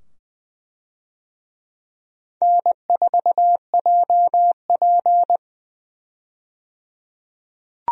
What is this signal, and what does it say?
Morse code, heard as a keyed tone, about 21 wpm, 700 Hz. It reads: N4JP
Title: USA Call Signs - 4 Characters - 20wpm